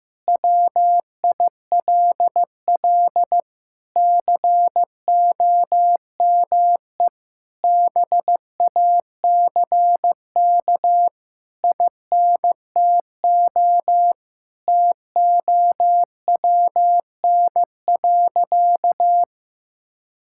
Morse code, heard as a keyed tone, about 15 wpm, 700 Hz. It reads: WILL COME BACK INTO TOWN.
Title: Sentences from Top 1000 Words - Review Part 6 - 15wpm